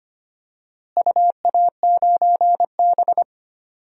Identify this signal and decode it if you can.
Morse code, heard as a keyed tone, about 25 wpm, 700 Hz. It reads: UA9B
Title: DXCC-100 Call Signs - 4 Characters - Repeated 2x - 25wpm